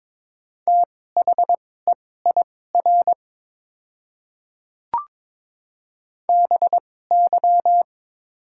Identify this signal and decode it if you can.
Morse code, heard as a keyed tone, about 22 wpm, spaced as for 16 wpm, 700 Hz. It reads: THEIR BY
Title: 2x Character Spacing - Top 100 Words - 22wpm